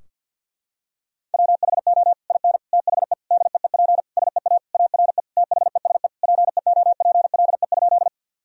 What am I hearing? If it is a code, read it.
Morse code, heard as a keyed tone, about 50 wpm, 700 Hz. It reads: WHO IN THE DEEP SEA ARE THESE PEOPLE?
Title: Sentences from Top 1000 Words - Review Part 3 - 50wpm